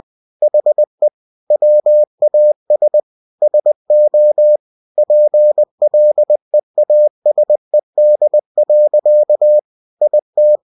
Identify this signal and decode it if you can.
Morse code, heard as a keyed tone, about 20 wpm, 600 Hz. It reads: HE WAS SO PLEASED. IT
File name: cw-books / winnie-the-pooh / winnie-20wpm-page-0109